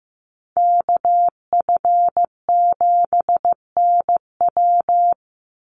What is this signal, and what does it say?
Morse code, heard as a keyed tone, about 15 wpm, 700 Hz. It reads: KF7NW
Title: USA Call Signs 15wpm